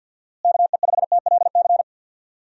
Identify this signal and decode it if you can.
Morse code, heard as a keyed tone, about 51 wpm, 700 Hz. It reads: KE5TLC
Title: USA Call Signs - 6 Characters - Repeated 3x - 50wpm